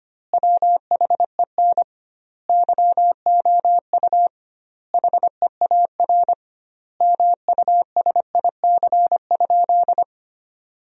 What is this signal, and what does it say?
Morse code, heard as a keyed tone, about 25 wpm, 700 Hz. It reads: WHEN YOU HEAR MUSIC?
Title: 2x Word-Spacing - Sentences from Top 300 Words - 25wpm